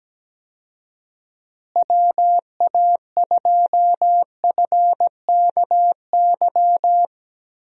Morse code, heard as a keyed tone, about 17 wpm, 700 Hz. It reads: WA2FKY
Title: USA Call Signs - 6 Characters - Repeated 3x - 17wpm